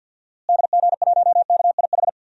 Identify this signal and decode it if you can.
Morse code, heard as a keyed tone, about 51 wpm, 700 Hz. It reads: DG1KIH
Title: DXCC-100 Call Signs - 6 Characters - Repeated 3x - 50wpm